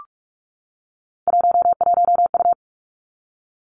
Morse code, heard as a keyed tone, about 46 wpm, 700 Hz. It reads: 11U
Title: Three Challenging Letter-Number Combinations 45wpm